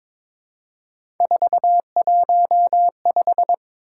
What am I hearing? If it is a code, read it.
Morse code, heard as a keyed tone, about 22 wpm, 700 Hz. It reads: 415
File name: practice / 22wpm / Three Letter-Number Combinations - 22wpm